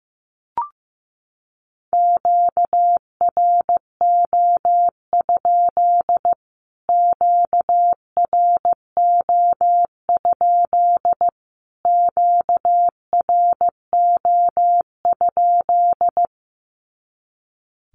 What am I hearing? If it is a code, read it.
Morse code, heard as a keyed tone, about 15 wpm, 700 Hz. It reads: QRO? QRO? QRO?
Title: Q-Codes Repeated 3x 15wpm